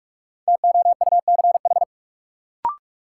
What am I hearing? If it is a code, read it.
Morse code, heard as a keyed tone, about 43 wpm, 700 Hz. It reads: TOUCH
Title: Sentences from Top 1200 Words - New Word Drill - 45wpm